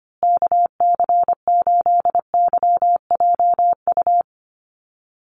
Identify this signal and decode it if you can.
Morse code, heard as a keyed tone, about 25 wpm, 700 Hz. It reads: KC8YJU
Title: USA Call Signs Repeated 3x 25wpm